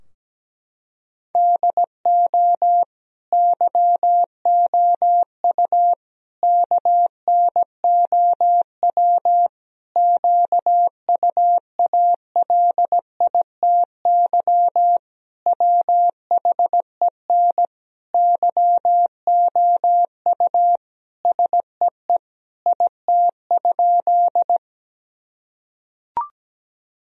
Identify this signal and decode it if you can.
Morse code, heard as a keyed tone, about 17 wpm, 700 Hz. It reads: DO YOU KNOW QUALITY WHEN YOU SEE IT?
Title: Sentences from Top 1000 Words - Review Part 4 - 17wpm